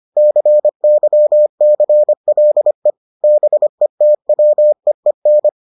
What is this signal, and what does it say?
Morse code, heard as a keyed tone, about 25 wpm, 600 Hz. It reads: CYCLE BETWEEN